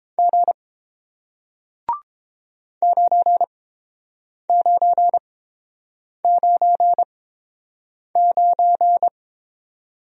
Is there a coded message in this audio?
Yes: Morse code from 0.2 to 9.1 s, heard as a keyed tone, about 30 wpm, spaced as for 7 wpm, 700 Hz. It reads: G 9999